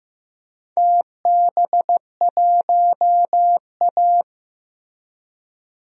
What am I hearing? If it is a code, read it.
Morse code, heard as a keyed tone, about 16 wpm, 700 Hz. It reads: TB1A
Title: DXCC-100 Call Signs - 4 Characters - Repeated 3x - 15wpm